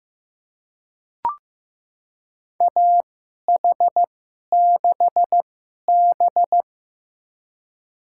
Morse code, heard as a keyed tone, about 15 wpm, spaced as for 11 wpm, 700 Hz. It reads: AH6B
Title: USA Call Signs - 4 Characters - 2x Character Spacing - 15wpm